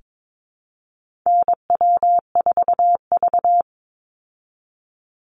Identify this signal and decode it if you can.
Morse code, heard as a keyed tone, about 22 wpm, 700 Hz. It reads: NW4V
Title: Call Signs Repeated 3x - 22wpm